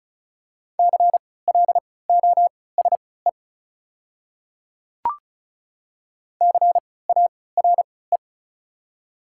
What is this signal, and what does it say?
Morse code, heard as a keyed tone, about 36 wpm, spaced as for 20 wpm, 700 Hz. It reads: CLOSE CARE
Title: 3x Character Spacing - Top 300 Words - 35wpm